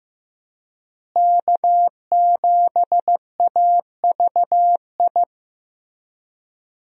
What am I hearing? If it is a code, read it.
Morse code, heard as a keyed tone, about 15 wpm, 700 Hz. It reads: K7AVI
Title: USA Call Signs - 5 Characters - Repeated 2x - 15wpm